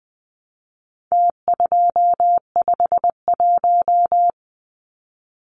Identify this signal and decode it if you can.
Morse code, heard as a keyed tone, about 20 wpm, 700 Hz. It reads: T251
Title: Numbers - Contest Style - 4-Digit 20wpm